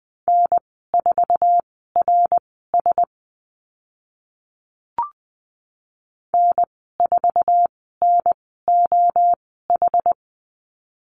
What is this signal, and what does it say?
Morse code, heard as a keyed tone, about 20 wpm, spaced as for 14 wpm, 700 Hz. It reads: N4RS N4NOH
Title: 069 Lesson 12g - Letter d - Call Signs 20wpm